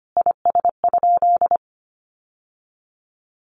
Morse code, heard as a keyed tone, about 25 wpm, 700 Hz. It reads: IS?
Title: Sentences from Top 1200 Words 25wpm